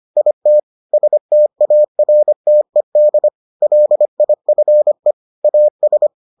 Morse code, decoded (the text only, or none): IT STARTED LIFE AS